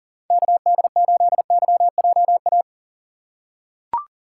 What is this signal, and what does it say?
Morse code, heard as a keyed tone, about 40 wpm, 700 Hz. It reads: KD8YJA
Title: USA Call Signs - 6 Characters - 2x Character Spacing - 40wpm